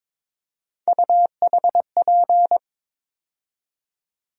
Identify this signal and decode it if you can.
Morse code, heard as a keyed tone, about 22 wpm, 700 Hz. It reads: UHP
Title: USA Call Sign Postfixes - 3 Letters 22wpm